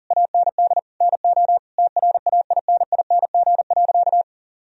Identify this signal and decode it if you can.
Morse code, heard as a keyed tone, about 40 wpm, 700 Hz. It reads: AND NO TRAINING.
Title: Sentences from Top 1000 Words - Review Part 4 - 40wpm